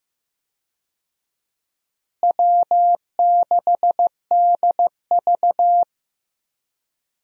Morse code, heard as a keyed tone, about 15 wpm, 700 Hz. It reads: W6DV
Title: USA Call Signs - 4 Characters - Repeated 2x - 15wpm